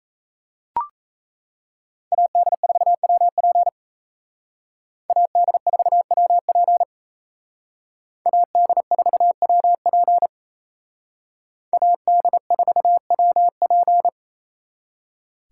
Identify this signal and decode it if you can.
Morse code, heard as a keyed tone, about 32 wpm, 700 Hz. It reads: AD4WP AD4WP AD4WP AD4WP
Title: Speed Racing - USA 2x2 Call Signs 28wpm